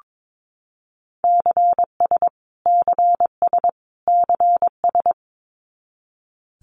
Morse code, heard as a keyed tone, about 22 wpm, 700 Hz. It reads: CS CS CS
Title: Abbreviations Repeated 3x - 22wpm